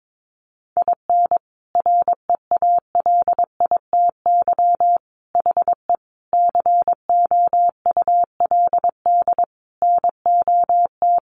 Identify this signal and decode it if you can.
Morse code, heard as a keyed tone, about 22 wpm, 700 Hz. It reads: IN REALITY HE COULD NOT
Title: Sentences from Top 1200 Words 22wpm